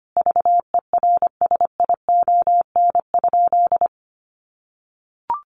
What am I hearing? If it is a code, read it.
Morse code, heard as a keyed tone, about 25 wpm, 700 Hz. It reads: VERSION?
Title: Sentences from Top 1400 Words 25wpm